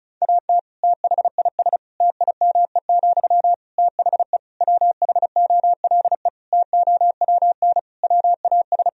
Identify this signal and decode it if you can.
Morse code, heard as a keyed tone, about 34 wpm, 700 Hz. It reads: AT THIS TIME, THE WHOLE TOWN WAS
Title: Sentences from Top 1000 Words - Review Part 2 - 35wpm